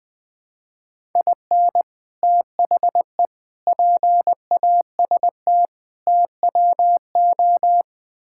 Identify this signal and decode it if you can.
Morse code, heard as a keyed tone, about 20 wpm, 700 Hz. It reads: IN THE PAST TWO
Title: Build A Phrase - 5-Word Phrases - 20wpm